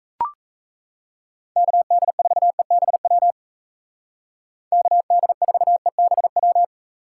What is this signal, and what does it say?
Morse code, heard as a keyed tone, about 36 wpm, 700 Hz. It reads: KD4EBW KD4EBW